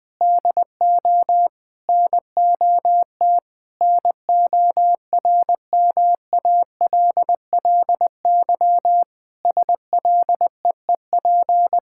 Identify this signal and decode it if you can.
Morse code, heard as a keyed tone, about 21 wpm, 700 Hz. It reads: DO NOT NORMALLY SLEEP
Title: Sentences from Top 1000 Words - Review Part 6 - 20wpm